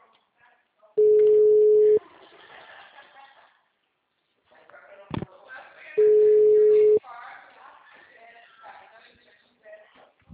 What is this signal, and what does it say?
Ringback tone (Europe, 425 Hz), with speech.
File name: out- -2012- - - .0